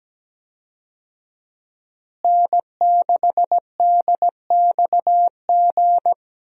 Morse code, heard as a keyed tone, about 17 wpm, 700 Hz. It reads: N6DXG
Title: USA Call Signs - 5 Characters - Repeated 3x - 17wpm